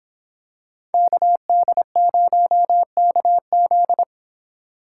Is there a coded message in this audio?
Yes: Morse code from 0.9 to 4.0 s, heard as a keyed tone, about 25 wpm, 700 Hz. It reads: KD0KZ